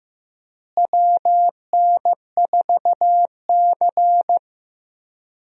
Morse code, heard as a keyed tone, about 15 wpm, 700 Hz. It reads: WN4C
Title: USA Call Signs - 4 Characters - Repeated 3x - 15wpm